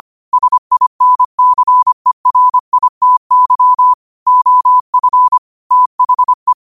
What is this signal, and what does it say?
Morse code, heard as a keyed tone, about 25 wpm, 1000 Hz. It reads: SINCERITY OF THE
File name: qotdfiles / 25wpm